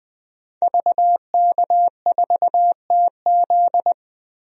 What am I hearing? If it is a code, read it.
Morse code, heard as a keyed tone, about 20 wpm, 700 Hz. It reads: VK4TZ
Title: DXCC-100 Call Signs - 5 Characters - 20wpm